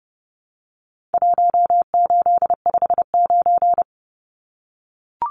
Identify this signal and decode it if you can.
Morse code, heard as a keyed tone, about 30 wpm, 700 Hz. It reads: 1859
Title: Numbers - 4-Digit 30wpm